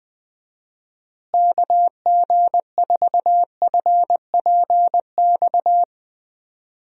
Morse code, heard as a keyed tone, about 20 wpm, 700 Hz. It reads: KG4FPX
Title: USA Call Signs - 6 Characters - Repeated 3x - 20wpm